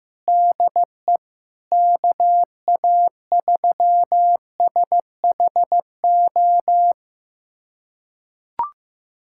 Morse code, heard as a keyed tone, about 16 wpm, 700 Hz. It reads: DE KA3SHO
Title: USA Callsign DE Callsign 15wpm